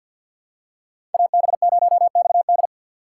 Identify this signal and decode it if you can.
Morse code, heard as a keyed tone, about 53 wpm, 700 Hz. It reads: AB0XD
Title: USA Call Signs - 5 Characters - 3x Character Spacing - 50wpm